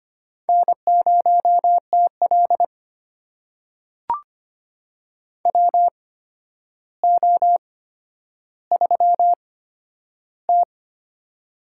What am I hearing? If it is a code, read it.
Morse code, heard as a keyed tone, about 25 wpm, spaced as for 11 wpm, 700 Hz. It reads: N0TL WO3T
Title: USA Call Signs - 4 Characters - 8x Character Spacing - 25wpm